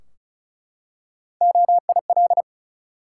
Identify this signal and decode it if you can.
Morse code, heard as a keyed tone, about 34 wpm, 700 Hz. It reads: OIL
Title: Three Letter Words - 35wpm